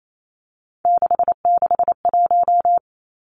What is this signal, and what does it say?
Morse code, heard as a keyed tone, about 28 wpm, 700 Hz. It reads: 661
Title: Numbers - 3-Digit 28wpm